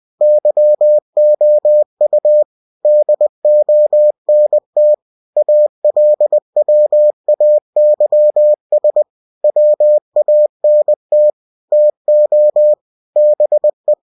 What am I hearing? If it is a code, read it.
Morse code, heard as a keyed tone, about 20 wpm, 600 Hz. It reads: YOU DONT ALWAYS WANT TO BE
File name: cw-books / winnie-the-pooh / winnie-20wpm-page-0047